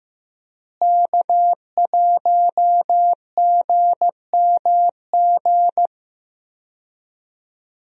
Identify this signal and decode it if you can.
Morse code, heard as a keyed tone, about 15 wpm, 700 Hz. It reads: K1GMG
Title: USA Call Signs - 5 Characters - Repeated 2x - 15wpm